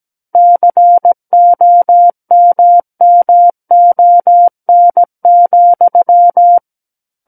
Morse code, heard as a keyed tone, about 17 wpm, 700 Hz. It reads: COMMON,
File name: Book One / 08-Chapter 8 - 17 WPM